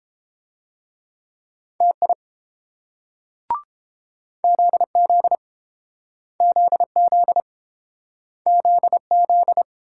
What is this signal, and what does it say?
Morse code, heard as a keyed tone, about 31 wpm, 700 Hz. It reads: TI ZZ ZZ ZZ